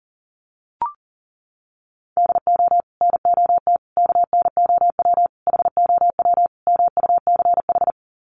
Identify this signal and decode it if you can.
Morse code, heard as a keyed tone, about 40 wpm, 700 Hz. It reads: DO NOT KNOW HOW MUCH
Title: Sets of 5 Words Encore 40wpm